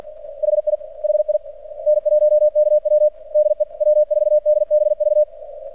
Morse code, heard as a keyed tone, about 50 wpm, 600 Hz, with radio noise. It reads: HI HI N1KW DE W4DBV
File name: archived / CRSnet